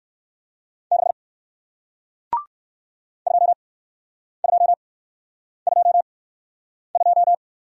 Morse code, heard as a keyed tone, about 56 wpm, spaced as for 9 wpm, 700 Hz. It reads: 6 2222
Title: Speed-Racing Single Letter-Number 45wpm